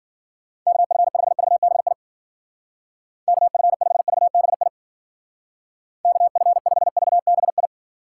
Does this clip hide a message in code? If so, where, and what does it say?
Yes: Morse code from 0.7 to 7.7 s, heard as a keyed tone, about 52 wpm, 700 Hz. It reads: KF5VBI KF5VBI KF5VBI